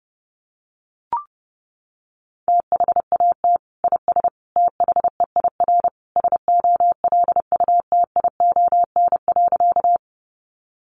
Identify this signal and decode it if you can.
Morse code, heard as a keyed tone, about 30 wpm, 700 Hz. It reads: THAT IS THEIR SOLUTION.
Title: Sentences from Top 1300 Words 30wpm